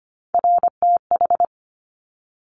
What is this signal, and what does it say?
Morse code, heard as a keyed tone, about 25 wpm, 700 Hz. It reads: RTH
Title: Top Three Letter Combinations in Words 25wpm